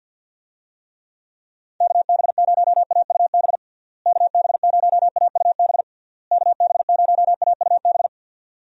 Morse code, heard as a keyed tone, about 52 wpm, 700 Hz. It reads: KB0AUB KB0AUB KB0AUB
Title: USA Call Signs - 6 Characters - Repeated 3x - 50wpm